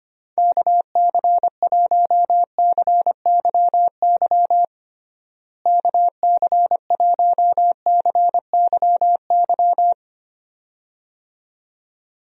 Morse code, heard as a keyed tone, about 25 wpm, 700 Hz. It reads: KC1CYY KC1CYY
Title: USA Call Signs - 6 Characters - Repeated 2x - 25wpm